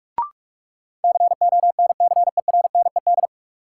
Morse code, heard as a keyed tone, about 46 wpm, 700 Hz. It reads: CONCERNED
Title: Sentences from Top 1000 Words - Review Part 6 - New Word Drill - 45wpm